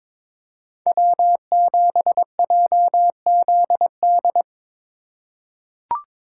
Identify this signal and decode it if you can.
Morse code, heard as a keyed tone, about 22 wpm, 700 Hz. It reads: W7JZD
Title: USA Call Signs - 5 Characters - 3x Character Spacing - 22wpm